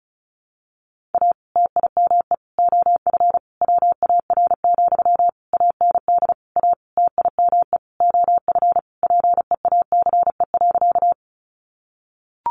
Morse code, heard as a keyed tone, about 34 wpm, 700 Hz. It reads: A TIME OF WAR, AND A TIME OF PEACE.
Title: Sentences from Top 1500 Words 35wpm